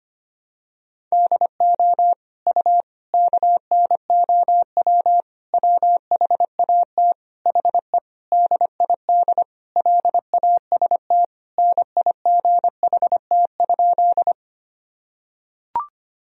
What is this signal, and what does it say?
Morse code, heard as a keyed tone, about 25 wpm, 700 Hz. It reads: DO U KNOW WHAT HE DID LAST NIGHT?
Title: Sentences from Top 300 Words and Abbreviations 25wpm